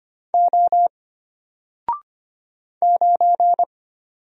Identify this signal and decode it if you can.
Morse code, heard as a keyed tone, about 25 wpm, 700 Hz. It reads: O 9
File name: Single-Letter-Number-25wpm